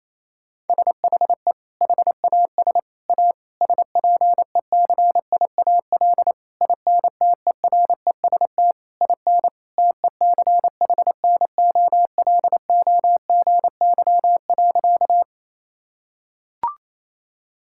Morse code, heard as a keyed tone, about 28 wpm, 700 Hz. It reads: SHE HAS A SPECIAL INTEREST IN TECHNOLOGY.
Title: Sentences from Top 1000 Words - Review Part 5 - 28wpm